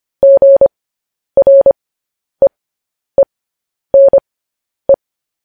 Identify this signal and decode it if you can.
Morse code, heard as a keyed tone, about 25 wpm, spaced as for 10 wpm, 550 Hz. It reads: GREENE